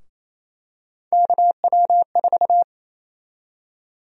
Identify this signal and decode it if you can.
Morse code, heard as a keyed tone, about 27 wpm, 700 Hz. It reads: KW4
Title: USA Call Sign Prefixes - Repeated 2x - 28wpm